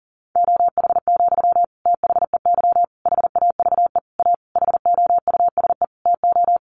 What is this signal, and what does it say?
Morse code, heard as a keyed tone, about 40 wpm, 700 Hz. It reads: OH, THEY HAVE A HOUSE TO
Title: Sentences from Top 800 Words 40wpm